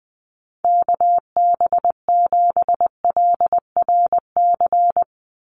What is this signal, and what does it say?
Morse code, heard as a keyed tone, about 20 wpm, 700 Hz. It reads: KB7LRC
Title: USA Call Signs 20wpm